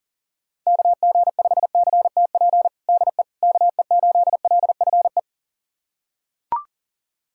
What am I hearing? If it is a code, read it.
Morse code, heard as a keyed tone, about 40 wpm, 700 Hz. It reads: KG5CTP DE KE8LFE
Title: USA Callsign DE Callsign - 6 Characters - 40wpm